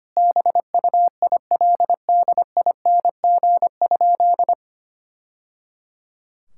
Morse code, heard as a keyed tone, about 25 wpm, 700 Hz. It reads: BUILDING?
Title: Sentences from Top 1000 Words - Review Part 5 - 25wpm